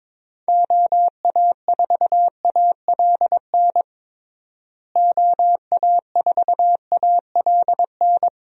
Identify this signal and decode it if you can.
Morse code, heard as a keyed tone, about 22 wpm, 700 Hz. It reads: OA4ALN OA4ALN
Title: USA Call Signs - 6 Characters - Repeated 2x - 22wpm